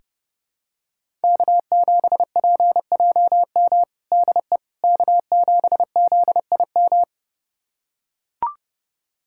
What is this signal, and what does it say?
Morse code, heard as a keyed tone, about 32 wpm, 700 Hz. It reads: K7PJM DE K7ZIM
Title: USA Callsign DE Callsign - 5 Characters - 30wpm